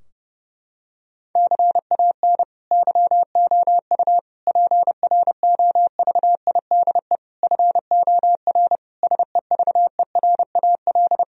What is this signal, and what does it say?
Morse code, heard as a keyed tone, about 30 wpm, 700 Hz. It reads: CAN YOU PROVIDE FOR SEVERAL
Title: Sentences from Top 1000 Words - Review Part 2 - 30wpm